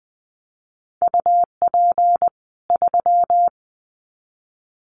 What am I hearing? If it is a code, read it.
Morse code, heard as a keyed tone, about 20 wpm, 700 Hz. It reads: UP 3
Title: Common QSO Phrases 20wpm